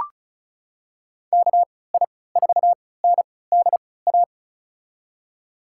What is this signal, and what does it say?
Morse code, heard as a keyed tone, about 36 wpm, spaced as for 20 wpm, 700 Hz. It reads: KI4NDA
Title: USA Call Signs - 3x Character Spacing - 35wpm